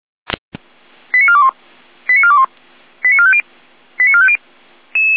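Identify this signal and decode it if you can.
Five-tone selective call (ZVEI), heard as quick runs of five tones, tones of about 70 ms. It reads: 89421 89421 89450 89450
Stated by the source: Five-tone ZVEI selective call received by SDR.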